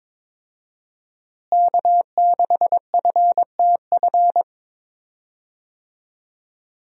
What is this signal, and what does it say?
Morse code, heard as a keyed tone, about 22 wpm, 700 Hz. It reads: K6FTF
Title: USA Call Signs - 5 Characters - Repeated 3x - 22wpm